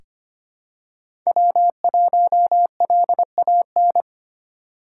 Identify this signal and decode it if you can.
Morse code, heard as a keyed tone, about 25 wpm, 700 Hz. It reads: W1LAN
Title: USA Call Signs - 5 Characters - 4x Character Spacing - 25wpm